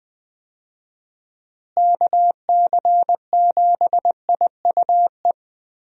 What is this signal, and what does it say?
Morse code, heard as a keyed tone, about 20 wpm, 700 Hz. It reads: KC7IUE
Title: USA Call Signs - 6 Characters - Repeated 3x - 20wpm